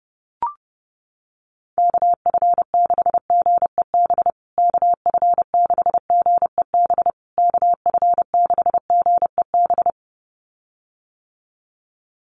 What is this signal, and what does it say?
Morse code, heard as a keyed tone, about 30 wpm, 700 Hz. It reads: KF6GEB KF6GEB KF6GEB
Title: USA Call Signs Repeated 3x 30wpm